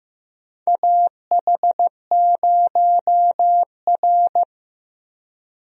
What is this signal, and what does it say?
Morse code, heard as a keyed tone, about 15 wpm, 700 Hz. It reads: AH0R
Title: USA Call Signs - 4 Characters - 15wpm